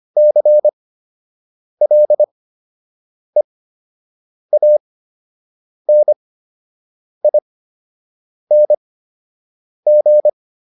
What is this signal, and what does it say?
Morse code, heard as a keyed tone, about 25 wpm, spaced as for 7 wpm, 600 Hz. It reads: CLEANING